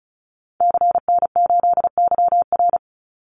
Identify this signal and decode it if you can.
Morse code, heard as a keyed tone, about 34 wpm, 700 Hz. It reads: CN8YR
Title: Call Signs Repeated 3x - 35wpm